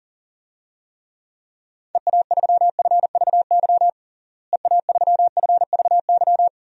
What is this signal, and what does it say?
Morse code, heard as a keyed tone, about 40 wpm, 700 Hz. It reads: EA3FVY EA3FVY
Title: DXCC-100 Call Signs - 6 Characters - Repeated 3x - 40wpm